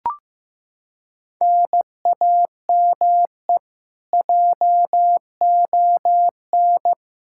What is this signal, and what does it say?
Morse code, heard as a keyed tone, about 15 wpm, 700 Hz. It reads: NAME JON
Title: K1USN SST QSO Practice 15wpm